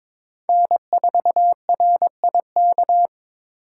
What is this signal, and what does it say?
Morse code, heard as a keyed tone, about 22 wpm, 700 Hz. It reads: N4RIK
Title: USA Call Signs - 5 Characters - 8x Character Spacing - 22wpm